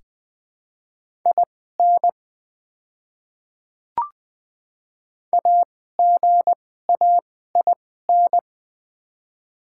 Morse code, heard as a keyed tone, about 20 wpm, spaced as for 14 wpm, 700 Hz. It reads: IN AGAIN
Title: 2x Character Spacing - Top 200 Words - 20wpm